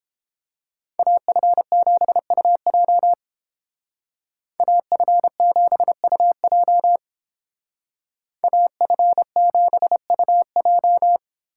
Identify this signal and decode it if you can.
Morse code, heard as a keyed tone, about 30 wpm, 700 Hz. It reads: AF7UJ AF7UJ AF7UJ